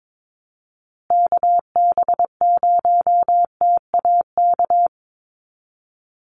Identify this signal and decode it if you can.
Morse code, heard as a keyed tone, about 22 wpm, 700 Hz. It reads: KB0TAK
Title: USA Call Signs Repeated 3x 22wpm